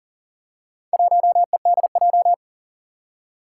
Morse code, heard as a keyed tone, about 43 wpm, 700 Hz. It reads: 1EDJ
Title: Four Challenging Letter-Number Combinations 40wpm